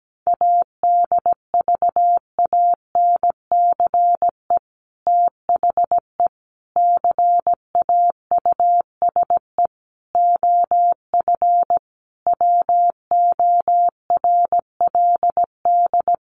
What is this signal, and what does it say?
Morse code, heard as a keyed tone, about 17 wpm, 700 Hz. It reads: ADVANCE THE CAUSE OF WORLD
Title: Sentences from Top 1400 Words 17wpm